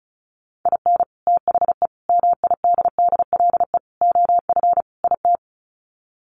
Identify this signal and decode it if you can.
Morse code, heard as a keyed tone, about 34 wpm, 700 Hz. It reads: IN THE MIDDLE OF IT
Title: Sets of 5 Words Encore 35wpm